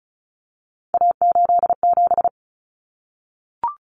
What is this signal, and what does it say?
Morse code, heard as a keyed tone, about 34 wpm, 700 Hz. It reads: A87
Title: Common Cut Numbers - 3-Digit 35wpm